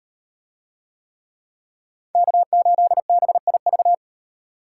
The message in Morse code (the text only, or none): K8BIV